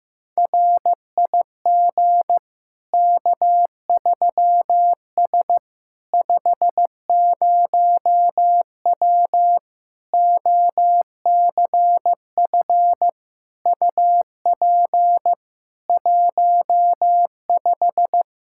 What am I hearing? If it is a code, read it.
Morse code, heard as a keyed tone, about 15 wpm, 700 Hz. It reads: RIG K3S 50W OCF UP 15
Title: K1USN SST QSO Practice 15wpm